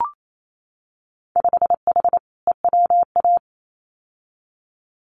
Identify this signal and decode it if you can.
Morse code, heard as a keyed tone, about 28 wpm, 700 Hz. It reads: 5H EWA
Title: Winter Field Day Exchanges Encore 28wpm